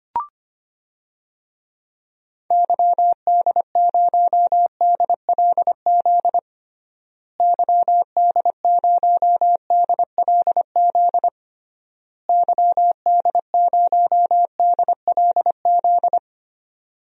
Morse code, heard as a keyed tone, about 25 wpm, 700 Hz. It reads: YD0DLZ YD0DLZ YD0DLZ
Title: DXCC-100 Call Signs - 6 Characters - Repeated 3x - 25wpm